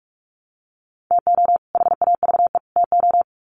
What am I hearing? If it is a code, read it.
Morse code, heard as a keyed tone, about 44 wpm, 700 Hz. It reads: TO HAVE TO
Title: Sets of 3 Words 45wpm